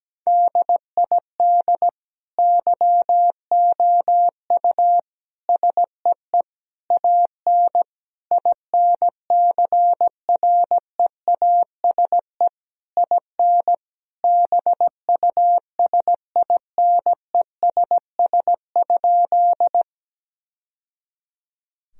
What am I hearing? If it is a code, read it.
Morse code, heard as a keyed tone, about 17 wpm, 700 Hz. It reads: DID YOU SEE AN INCREASE IN BUSINESS?
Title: Sentences from Top 1000 Words - Review Part 4 - 17wpm